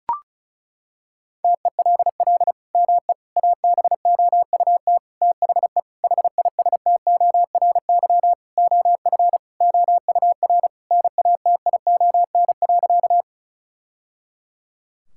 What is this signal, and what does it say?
Morse code, heard as a keyed tone, about 36 wpm, 700 Hz. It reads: TELL ME ABOUT THE HISTORY OF OUR NATION.
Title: Sentences from Top 1000 Words - Review Part 6 - 35wpm